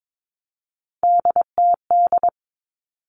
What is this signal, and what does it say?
Morse code, heard as a keyed tone, about 22 wpm, 700 Hz. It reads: DTD
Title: All Cut Numbers - 3-Digit 22wpm